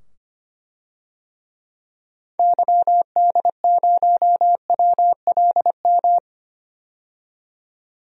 Morse code, heard as a keyed tone, about 25 wpm, 700 Hz. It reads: YD0WLM
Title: DXCC-100 Call Signs - 6 Characters - 25wpm